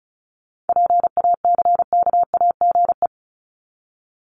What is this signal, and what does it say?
Morse code, heard as a keyed tone, about 34 wpm, 700 Hz. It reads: PACKAGE
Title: Top 1501-1750 Words 35wpm